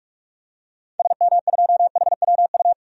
Morse code, heard as a keyed tone, about 44 wpm, 700 Hz. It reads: SM2HWU
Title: DXCC-100 Call Signs - 6 Characters - 45wpm